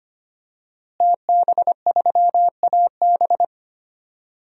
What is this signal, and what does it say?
Morse code, heard as a keyed tone, about 25 wpm, 700 Hz. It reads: TB3AB
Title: DXCC-100 Call Signs - 5 Characters - Repeated 3x - 25wpm